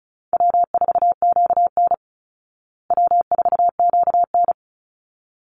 Morse code, heard as a keyed tone, about 34 wpm, 700 Hz. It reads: W4QN W4QN
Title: Easy Call Signs Repeated 2x - 35wpm